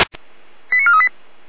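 Five-tone selective call (ZVEI), heard as quick runs of five tones, tones of about 80 ms. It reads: 89438
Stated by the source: Five-tone ZVEI selective call received on SDR.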